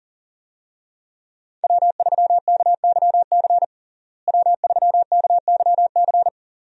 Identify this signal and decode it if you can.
Morse code, heard as a keyed tone, about 40 wpm, 700 Hz. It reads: W3KYC W3KYC
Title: USA Call Signs - 5 Characters - Repeated 2x - 40wpm